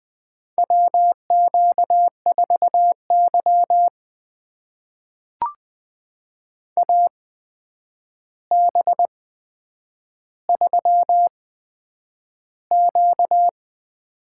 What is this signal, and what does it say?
Morse code, heard as a keyed tone, about 20 wpm, spaced as for 9 wpm, 700 Hz. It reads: WQ4Y AB3Q
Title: USA Call Signs - 8x Character Spacing - 20wpm